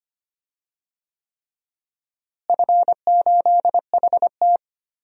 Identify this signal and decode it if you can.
Morse code, heard as a keyed tone, about 25 wpm, 700 Hz. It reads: F8HT
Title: DXCC-100 Call Signs - 4 Characters - Repeated 3x - 25wpm